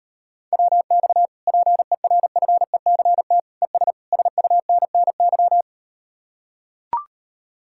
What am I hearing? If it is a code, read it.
Morse code, heard as a keyed tone, about 38 wpm, 700 Hz. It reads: WX PERFECT ES SUNNY